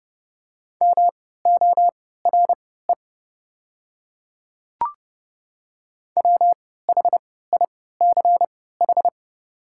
Morse code, heard as a keyed tone, about 30 wpm, spaced as for 17 wpm, 700 Hz. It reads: MORE WHICH
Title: 3x Character Spacing - Top 200 Words - 30wpm